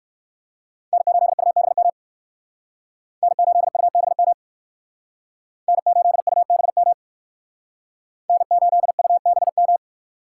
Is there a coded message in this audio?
Yes: Morse code from 0.9 to 9.8 s, heard as a keyed tone, about 57 wpm, 700 Hz. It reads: N8UBM N8UBM N8UBM N8UBM